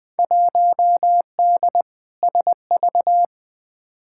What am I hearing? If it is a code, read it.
Morse code, heard as a keyed tone, about 20 wpm, 700 Hz. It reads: 1D SV
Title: Field Day Exchanges - 20wpm